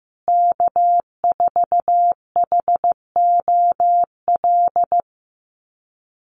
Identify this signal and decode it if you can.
Morse code, heard as a keyed tone, about 15 wpm, 700 Hz. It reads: K4HOL
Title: USA Call Signs Repeated 3x 15wpm